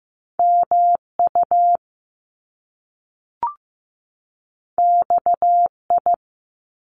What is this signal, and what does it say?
Morse code, heard as a keyed tone, about 15 wpm, 700 Hz. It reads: MU XI
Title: Top Two Letter Combinations in Words 15wpm